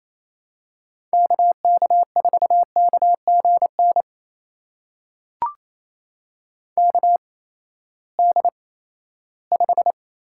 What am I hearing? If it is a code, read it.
Morse code, heard as a keyed tone, about 28 wpm, spaced as for 16 wpm, 700 Hz. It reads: KK4KGN KD5
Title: USA Call Signs - 6 Characters - 8x Character Spacing - 28wpm